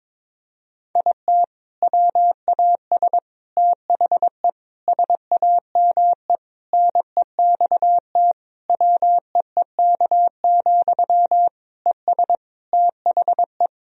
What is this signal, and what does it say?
Morse code, heard as a keyed tone, about 22 wpm, 700 Hz. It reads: IT WAS THE SAME NEXT WEEK, ES THE
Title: Sentences from Top 500 Words and Abbreviations 22wpm